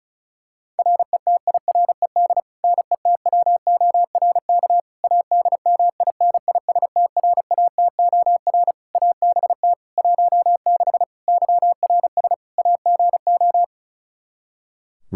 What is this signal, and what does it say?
Morse code, heard as a keyed tone, about 36 wpm, 700 Hz. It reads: RETIRED NETWORK ADMINISTRATOR ABT 16 YRS AGO
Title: Practice QSO - Exchange 3 - More Info - 35wpm